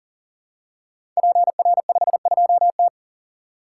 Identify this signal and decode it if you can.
Morse code, heard as a keyed tone, about 41 wpm, 700 Hz. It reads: PR52T
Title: Code Groups - 5 Letter:Numbers - 40wpm